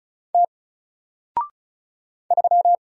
Morse code, heard as a keyed tone, about 35 wpm, 700 Hz. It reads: T 3
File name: Single-Letter-Number-35wpm